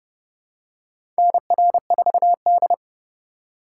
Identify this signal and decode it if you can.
Morse code, heard as a keyed tone, about 30 wpm, 700 Hz. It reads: NR4D
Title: USA Call Signs - 4 Characters - 4x Character Spacing - 30wpm